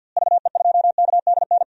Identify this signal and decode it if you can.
Morse code, heard as a keyed tone, about 50 wpm, 700 Hz. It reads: VE2KDN
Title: DXCC-100 Call Signs - 6 Characters - Repeated 3x - 50wpm